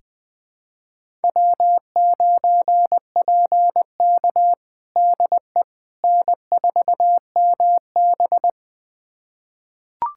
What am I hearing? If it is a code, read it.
Morse code, heard as a keyed tone, about 20 wpm, 700 Hz. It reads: W9PK DE N4MB
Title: USA Callsign DE Callsign - 4 Characters - 20wpm